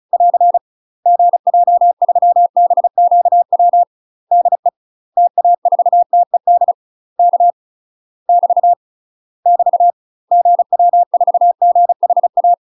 Morse code, heard as a keyed tone, about 36 wpm, 700 Hz. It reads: <AR> GJ3BQW DE TA4TED K <BT> <BT> GW4GHA
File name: UK MASTER QSOs TRAINING / 35WPM / UK MASTER QSOs 0038